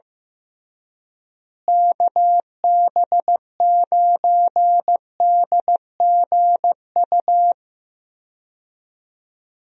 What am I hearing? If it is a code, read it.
Morse code, heard as a keyed tone, about 16 wpm, 700 Hz. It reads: KB9DGU
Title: USA Call Signs - 6 Characters - Repeated 3x - 15wpm